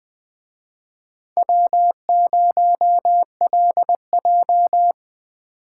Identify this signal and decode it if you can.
Morse code, heard as a keyed tone, about 20 wpm, 700 Hz. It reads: W0LJ
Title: Speed Racing - USA 1x2 Call Signs 17wpm